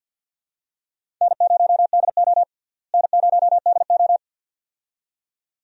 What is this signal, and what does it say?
Morse code, heard as a keyed tone, about 48 wpm, 700 Hz. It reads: N0DO N0DO
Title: USA Call Signs - 4 Characters - Repeated 2x - 50wpm